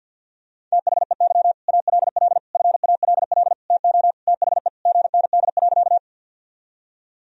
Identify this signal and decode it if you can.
Morse code, heard as a keyed tone, about 52 wpm, 700 Hz. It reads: THEY ALL FALL TO THE GND.
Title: Sentences from Top 500 Words and Abbreviations 50wpm